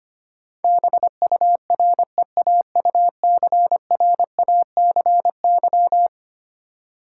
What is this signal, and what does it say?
Morse code, heard as a keyed tone, about 25 wpm, 700 Hz. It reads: BUREAUCRACY
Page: morse-code-ninja-practice-25wpm